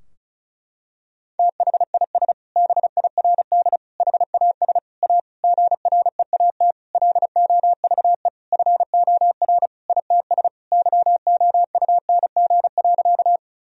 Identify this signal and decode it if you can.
Morse code, heard as a keyed tone, about 36 wpm, 700 Hz. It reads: THIS BIRD HAS A GREAT LOVE FOR ITS YOUNG.
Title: Sentences from Top 1000 Words - Review Part 2 - 35wpm